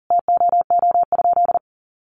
Morse code, heard as a keyed tone, about 40 wpm, 700 Hz. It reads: TOO?
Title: Sentences from Top 500 Words 40wpm